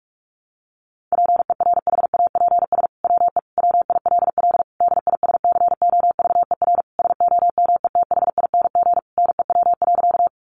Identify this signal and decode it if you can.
Morse code, heard as a keyed tone, about 44 wpm, 700 Hz. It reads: PERHAPS WE WILL DISCOVER SOMETHING NEW.
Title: Sentences from Top 1100 Words 45wpm